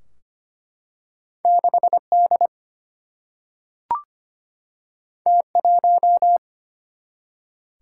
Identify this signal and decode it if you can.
Morse code, heard as a keyed tone, about 25 wpm, 700 Hz. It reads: 6 D T 1
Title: Single-Letter Single-Number 25wpm